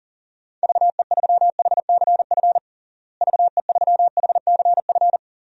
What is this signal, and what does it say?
Morse code, heard as a keyed tone, about 40 wpm, 700 Hz. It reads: VE3HCF VE3HCF
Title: DXCC-100 Call Signs - 6 Characters - Repeated 3x - 40wpm